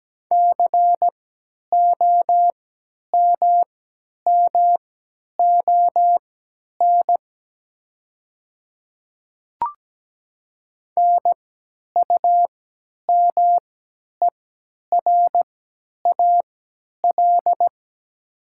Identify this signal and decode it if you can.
Morse code, heard as a keyed tone, about 17 wpm, spaced as for 10 wpm, 700 Hz. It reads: COMMON NUMERAL
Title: 3x Character Spacing - Top 500 Words - 17wpm